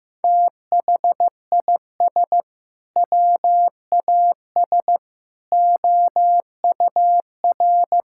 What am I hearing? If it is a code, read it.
Morse code, heard as a keyed tone, about 15 wpm, 700 Hz. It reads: THIS WAS OUR
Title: Sentences from Top 1000 Words - Review Part 5 - 15wpm